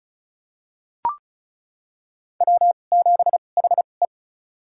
Morse code, heard as a keyed tone, about 34 wpm, spaced as for 25 wpm, 700 Hz. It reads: W7HE
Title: USA Call Signs - 4 Characters - 2x Character Spacing - 35wpm